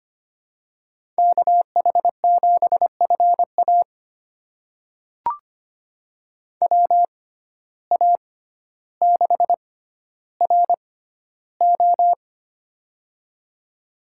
Morse code, heard as a keyed tone, about 25 wpm, spaced as for 13 wpm, 700 Hz. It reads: KH7FA WA6RO
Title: USA Call Signs - 5 Characters - 6x Character Spacing - 25wpm